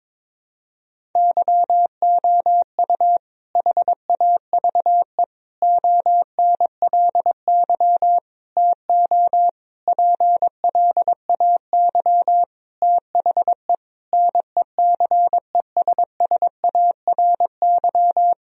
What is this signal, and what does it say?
Morse code, heard as a keyed tone, about 22 wpm, 700 Hz. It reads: YOU HAVE ONLY TO PLAY THE NECESSARY
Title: Sentences from Top 1000 Words - Review Part 1 - 22wpm